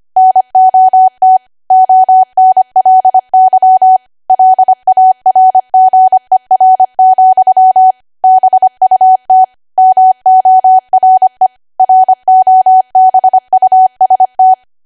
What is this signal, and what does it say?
Morse code, heard as a keyed tone, about 25 wpm, 750 Hz. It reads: NOT ONLY LARGER, BUT MORE ROBUST
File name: Archive / 25 WPM / 25WPM